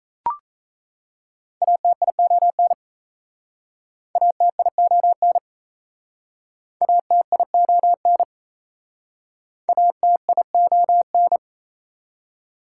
Morse code, heard as a keyed tone, about 35 wpm, 700 Hz. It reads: ATION ATION ATION ATION